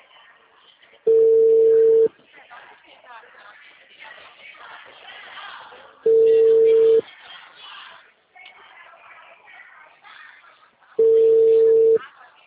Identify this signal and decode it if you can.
Ringback tone (China, 450 Hz), with speech.
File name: out- -2014